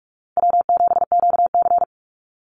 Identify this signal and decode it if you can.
Morse code, heard as a keyed tone, about 44 wpm, 700 Hz. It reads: W7QC
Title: Call Signs Repeated 2x - 45wpm